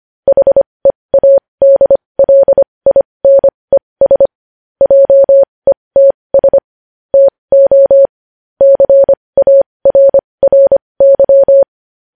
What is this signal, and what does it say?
Morse code, heard as a keyed tone, about 25 wpm, spaced as for 20 wpm, 550 Hz. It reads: HEADLINES JETS TO CARRY
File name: podcast / News headlines-1x-25x20